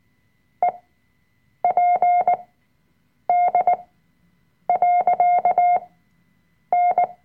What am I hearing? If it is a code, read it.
Morse code, heard as a keyed tone, about 18 wpm, spaced as for 7 wpm, 710 Hz. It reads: EPD.N
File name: Workshop Dateien / CW-Online-Kurs / Lektion 005